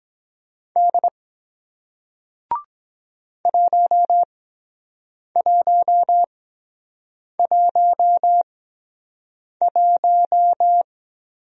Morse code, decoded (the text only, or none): D 1111